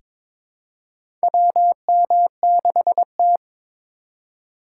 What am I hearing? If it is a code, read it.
Morse code, heard as a keyed tone, about 22 wpm, 700 Hz. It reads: WM6T
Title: USA Call Signs - 3x Character Spacing - 22wpm